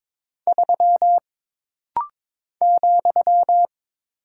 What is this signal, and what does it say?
Morse code, heard as a keyed tone, about 22 wpm, 700 Hz. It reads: 3 ,
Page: This rapid-fire timing is geared to push your ICR (Instant Character Recognition) skills.